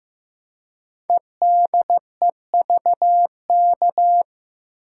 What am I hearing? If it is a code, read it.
Morse code, heard as a keyed tone, about 15 wpm, 700 Hz. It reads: EDEVK
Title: Code Groups - 5 Letters - 15wpm